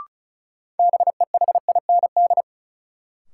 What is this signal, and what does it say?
Morse code, heard as a keyed tone, about 36 wpm, 700 Hz. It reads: BEHIND
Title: Sentences from Top 500 Words - New Word Drill - 35wpm